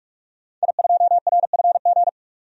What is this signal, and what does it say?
Morse code, heard as a keyed tone, about 42 wpm, 700 Hz. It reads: I2RFG
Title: DXCC-100 Call Signs - 5 Characters - 45wpm